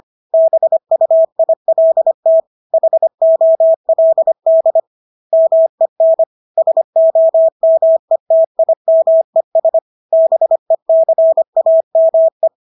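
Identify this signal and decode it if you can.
Morse code, heard as a keyed tone, about 25 wpm, 650 Hz. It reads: BUILT HOLD MEN SOMETIMES BECAME
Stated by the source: Morse practice of English words at 25 wpm from CW med Gnister.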